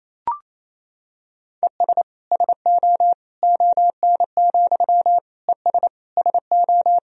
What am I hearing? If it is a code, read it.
Morse code, heard as a keyed tone, about 28 wpm, 700 Hz. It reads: ES SO ON, ES SO
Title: Sentences from Top 100 Words and Abbreviations 28wpm